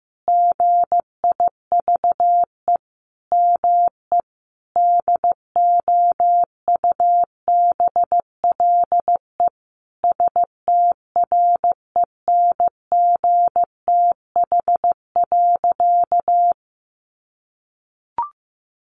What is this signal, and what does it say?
Morse code, heard as a keyed tone, about 15 wpm, 700 Hz. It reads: GIVE ME DOUBLE STRENGTH.
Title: Sentences from Top 1400 Words 15wpm